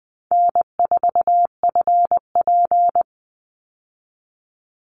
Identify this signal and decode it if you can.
Morse code, heard as a keyed tone, about 20 wpm, 700 Hz. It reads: N4FP
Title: Easy Call Signs Repeated 2x - 20wpm